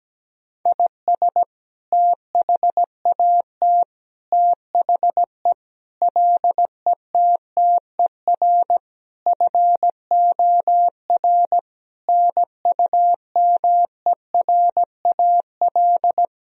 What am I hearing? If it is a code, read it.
Morse code, heard as a keyed tone, about 17 wpm, 700 Hz. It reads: IS THAT THE LETTER FOR NUMERAL
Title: Sentences from Top 1000 Words - Review Part 2 - 17wpm